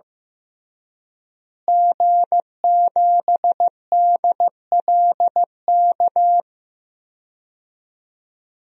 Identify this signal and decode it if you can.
Morse code, heard as a keyed tone, about 15 wpm, 700 Hz. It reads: G7DLK
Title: DXCC-100 Call Signs - 5 Characters - Repeated 2x - 15wpm